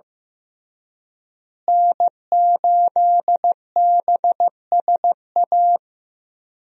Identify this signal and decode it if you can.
Morse code, heard as a keyed tone, about 16 wpm, 700 Hz. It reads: N8BSA
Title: USA Call Signs - 5 Characters - Repeated 3x - 15wpm